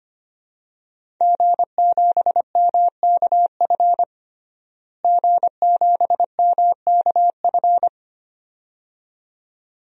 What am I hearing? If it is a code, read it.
Morse code, heard as a keyed tone, about 25 wpm, 700 Hz. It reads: G7MKF G7MKF
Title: DXCC-100 Call Signs - 5 Characters - Repeated 2x - 25wpm